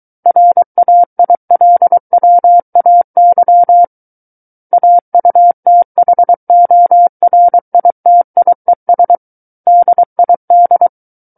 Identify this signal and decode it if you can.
Morse code, heard as a keyed tone, about 23 wpm, 700 Hz. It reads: RAILWAY AUTHORITIES DID
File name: Book One / 14-Chapter 14 - 23 WPM